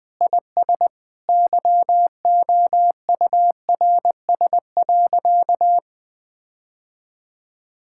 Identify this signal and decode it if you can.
Morse code, heard as a keyed tone, about 20 wpm, 700 Hz. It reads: IS YOURS.
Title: Sentences with 8-Word Phrases - 20wpm